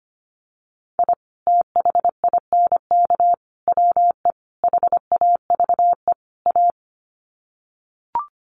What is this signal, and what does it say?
Morse code, heard as a keyed tone, about 25 wpm, 700 Hz. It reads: I THINK WE HAVE A
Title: Sets of 5 Words Encore 25wpm